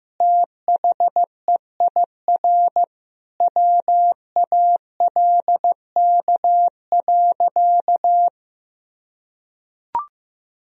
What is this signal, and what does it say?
Morse code, heard as a keyed tone, about 16 wpm, 700 Hz. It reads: THEIR WALK.
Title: Sentences from Top 1000 Words - Review Part 2 - 15wpm